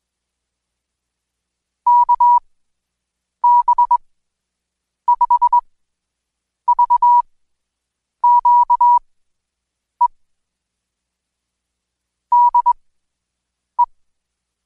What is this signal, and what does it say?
Morse code, heard as a keyed tone, about 20 wpm, spaced as for 7 wpm, 970 Hz. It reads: KB5VQE DE
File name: Code Quick / AudioDownld2 / Seg112-Day17-DayDrill 17